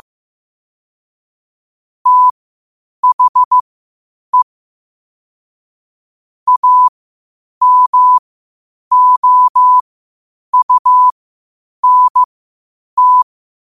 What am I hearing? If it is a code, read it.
Morse code, heard as a keyed tone, about 15 wpm, spaced as for 8 wpm, 1000 Hz. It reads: THE AMOUNT
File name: qotdfiles / 8wpm